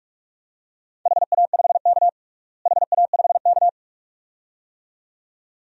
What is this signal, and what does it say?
Morse code, heard as a keyed tone, about 46 wpm, 700 Hz. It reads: HA5K HA5K
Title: DXCC-100 Call Signs - 4 Characters - Repeated 2x - 45wpm